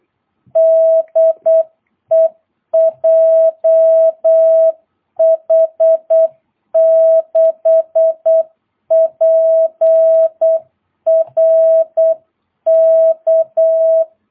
Morse code, heard as a keyed tone, about 8 wpm, 650 Hz. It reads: DE JH6PRK
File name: jh8cbh / 2022cw